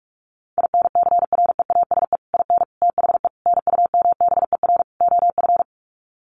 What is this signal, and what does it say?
Morse code, heard as a keyed tone, about 44 wpm, 700 Hz. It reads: INCREASE IN THE NUMBER OF
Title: Sets of 5 Words 45wpm